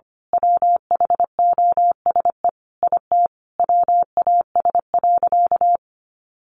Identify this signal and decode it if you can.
Morse code, heard as a keyed tone, about 25 wpm, 700 Hz. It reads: WHOSE IT WAS.
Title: Sentences from Top 1200 Words 25wpm